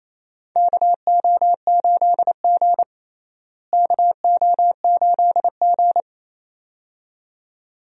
Morse code, heard as a keyed tone, about 27 wpm, 700 Hz. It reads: KO8G KO8G
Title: USA Call Signs - 4 Characters - Repeated 2x - 28wpm